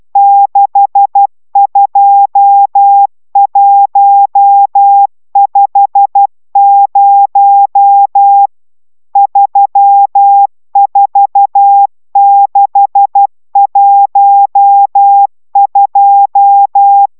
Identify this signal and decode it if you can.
Morse code, heard as a keyed tone, about 12 wpm, 800 Hz. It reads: 62150 34612